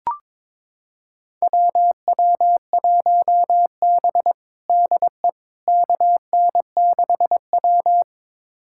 Morse code, heard as a keyed tone, about 22 wpm, 700 Hz. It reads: WW1B DE KN6W
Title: USA Callsign DE Callsign - 4 Characters - 22wpm